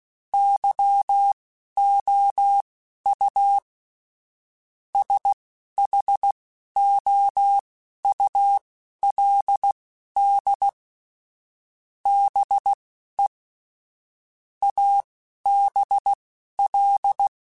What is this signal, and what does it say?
Morse code, heard as a keyed tone, about 16 wpm, spaced as for 11 wpm, 800 Hz. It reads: YOU SHOULD BE ABL
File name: Technical Subjects / Technical Subjects-mp3s / OR Gate @10